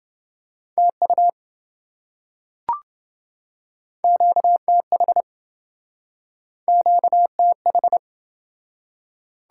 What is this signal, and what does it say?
Morse code, heard as a keyed tone, about 27 wpm, 700 Hz. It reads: TU QTH QTH